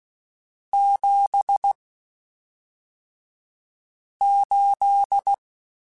Morse code, heard as a keyed tone, about 16 wpm, spaced as for 3 wpm, 800 Hz. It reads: 78
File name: (spaced) / QSOs Spaced-mp3s / kc7sac @5 sp